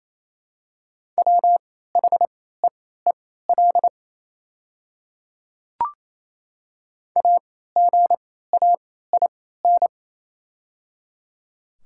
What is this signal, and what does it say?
Morse code, heard as a keyed tone, about 28 wpm, spaced as for 16 wpm, 700 Hz. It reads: WHEEL AGAIN
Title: 3x Character Spacing - Top 500 Words - 28wpm